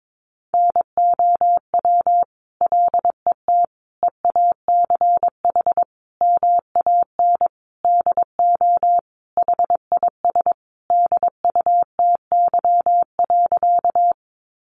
Morse code, heard as a keyed tone, about 22 wpm, 700 Hz. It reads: NOW LET EACH MAN DO HIS DUTY.
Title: Sentences from Top 1500 Words 22wpm